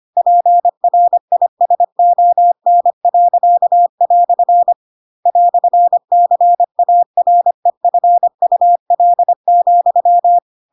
Morse code, heard as a keyed tone, about 25 wpm, 700 Hz. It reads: PRISON." "CAREFUL,
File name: Pinocchio / Pinocchio ch4 25 25 ik1hge